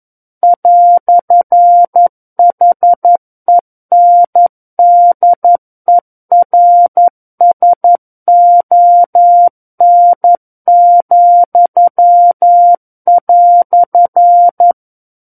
Morse code, heard as a keyed tone, about 11 wpm, 700 Hz. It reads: "HENDERSON,"
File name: 11 WPM